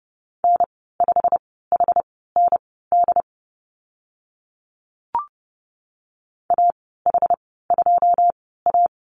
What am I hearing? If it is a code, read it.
Morse code, heard as a keyed tone, about 30 wpm, spaced as for 17 wpm, 700 Hz. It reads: N5HND AH2A
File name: Lesson 15g - Number 5 - Call Signs 30wpm